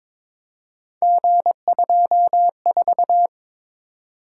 Morse code, heard as a keyed tone, about 22 wpm, 700 Hz. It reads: G24
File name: practice / 22wpm / Three Letter-Number Combinations - 22wpm